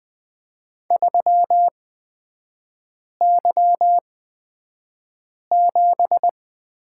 Morse code, heard as a keyed tone, about 20 wpm, spaced as for 5 wpm, 700 Hz. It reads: 3Y7
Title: 224 Lesson 36e - BT - Three Characters 20wpm